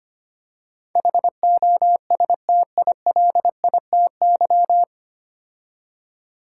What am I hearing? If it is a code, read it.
Morse code, heard as a keyed tone, about 25 wpm, 700 Hz. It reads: HOSTILITY
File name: practice / 25wpm / 9 Letter Words - First 3 Letters Given - 25wpm